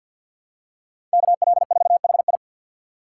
Morse code, heard as a keyed tone, about 51 wpm, 700 Hz. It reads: KL4HI
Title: USA Call Signs - 5 Characters - 2x Character Spacing - 50wpm